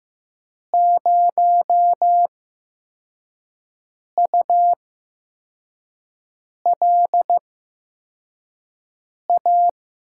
Morse code, heard as a keyed tone, about 15 wpm, spaced as for 4 wpm, 700 Hz. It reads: 0ULA
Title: USA Call Signs - 8x Character Spacing - 15wpm